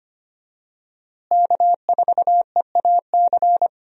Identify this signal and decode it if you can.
Morse code, heard as a keyed tone, about 25 wpm, 700 Hz. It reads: K4EAC
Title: USA Call Signs - 5 Characters - Repeated 3x - 25wpm